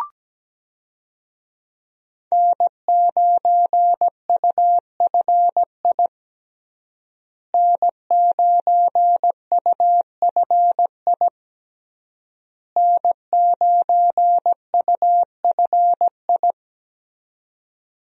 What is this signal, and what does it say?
Morse code, heard as a keyed tone, about 17 wpm, 700 Hz. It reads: N9UFI N9UFI N9UFI
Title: USA Call Signs - 5 Characters - Repeated 3x - 17wpm